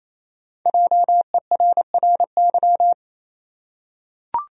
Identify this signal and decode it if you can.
Morse code, heard as a keyed tone, about 27 wpm, 700 Hz. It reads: JERRY
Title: USA Common Names - Max 5 Letters - 28wpm